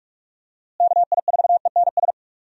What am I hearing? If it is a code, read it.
Morse code, heard as a keyed tone, about 45 wpm, 700 Hz. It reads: KI4ENS
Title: USA Call Signs - 6 Characters - Repeated 3x - 45wpm